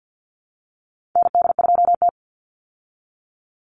Morse code, heard as a keyed tone, about 50 wpm, 700 Hz. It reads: ND2T
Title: Call Signs Repeated 2x - 50wpm